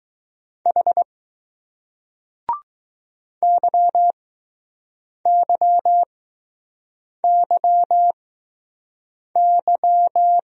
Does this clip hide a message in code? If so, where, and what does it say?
Yes: Morse code from 0.7 to 10.4 s, heard as a keyed tone, about 20 wpm, spaced as for 6 wpm, 700 Hz. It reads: H YYYY